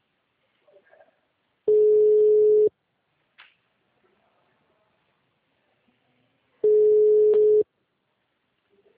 Ringback tone (Europe, 425 Hz).